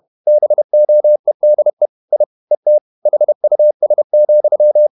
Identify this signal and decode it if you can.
Morse code, heard as a keyed tone, about 31 wpm, 600 Hz. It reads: BOEDE I ET HUS,
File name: Ferie QRQ 31wpm